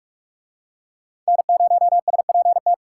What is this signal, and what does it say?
Morse code, heard as a keyed tone, about 44 wpm, 700 Hz. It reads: N0SPT